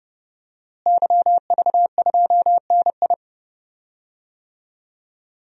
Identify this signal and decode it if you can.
Morse code, heard as a keyed tone, about 30 wpm, 700 Hz. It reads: YV2NI
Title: DXCC-100 Call Signs - 5 Characters - Repeated 2x - 30wpm